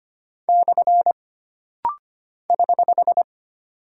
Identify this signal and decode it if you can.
Morse code, heard as a keyed tone, about 25 wpm, 700 Hz. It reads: / <HH>
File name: Single Punctuation-Prosign - Rapid-Fire - 25wpm